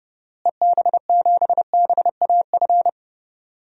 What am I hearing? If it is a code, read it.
Morse code, heard as a keyed tone, about 30 wpm, 700 Hz. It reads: EB7BAF
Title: DXCC-100 Call Signs - 6 Characters - Repeated 3x - 30wpm